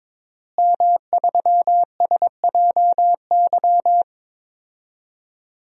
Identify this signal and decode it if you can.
Morse code, heard as a keyed tone, about 22 wpm, 700 Hz. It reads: M3SJY
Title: DXCC-100 Call Signs - 5 Characters - Repeated 3x - 22wpm